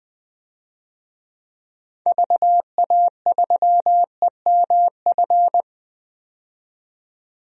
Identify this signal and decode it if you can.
Morse code, heard as a keyed tone, about 21 wpm, 700 Hz. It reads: VA3EMF
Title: DXCC-100 Call Signs - 6 Characters - 20wpm